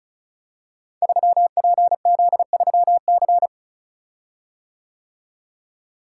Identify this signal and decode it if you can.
Morse code, heard as a keyed tone, about 36 wpm, 700 Hz. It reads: 3PZ3C
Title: Code Groups - 5 Letter:Numbers - 35wpm